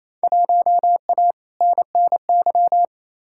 Morse code, heard as a keyed tone, about 28 wpm, 700 Hz. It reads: 1A NNY
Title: Field Day Exchanges Encore - 28wpm